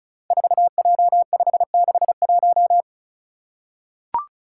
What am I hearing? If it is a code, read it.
Morse code, heard as a keyed tone, about 36 wpm, 700 Hz. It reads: 4J561
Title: Five Challenging Letter-Number Combinations 35wpm